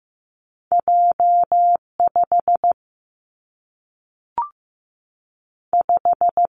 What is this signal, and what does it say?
Morse code, heard as a keyed tone, about 15 wpm, 700 Hz. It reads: J5 5
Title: Two Challenging Letter-Number Combinations 15wpm